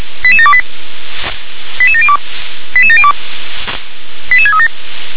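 Five-tone selective call (ZVEI), heard as quick runs of five tones, tones of about 70 ms. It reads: 88728 88702 88702 88637
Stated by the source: ZVEI five-tone selective call 88730 received on an SDR.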